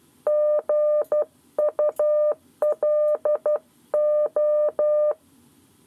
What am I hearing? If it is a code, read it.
Morse code, heard as a keyed tone, about 12 wpm, 600 Hz. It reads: GULO
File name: 7 Lezione cw 1